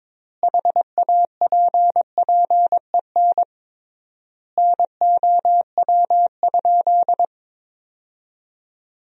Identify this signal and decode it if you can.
Morse code, heard as a keyed tone, about 22 wpm, 700 Hz. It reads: HAPPEN NOW?
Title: 3x Word-Spacing - Sentences from Top 400 Words - 22wpm